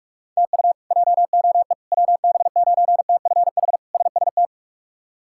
Morse code, heard as a keyed tone, about 46 wpm, 700 Hz. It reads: TU JOE WB9TFH SST
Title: K1USN SST QSO Practice - 45wpm